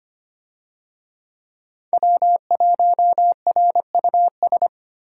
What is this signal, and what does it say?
Morse code, heard as a keyed tone, about 25 wpm, 700 Hz. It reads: W1RUS
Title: USA Call Signs - 5 Characters - Repeated 2x - 25wpm